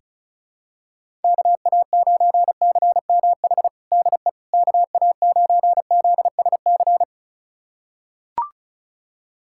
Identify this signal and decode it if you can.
Morse code, heard as a keyed tone, about 34 wpm, 700 Hz. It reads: KA9CMH DE KA9ZSC
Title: USA Callsign DE Callsign 35wpm